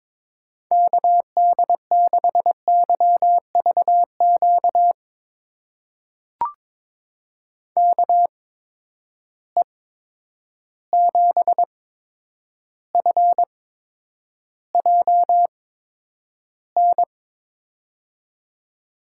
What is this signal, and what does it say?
Morse code, heard as a keyed tone, about 22 wpm, spaced as for 9 wpm, 700 Hz. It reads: KD6YVQ KE7FJN
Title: USA Call Signs - 6 Characters - 8x Character Spacing - 22wpm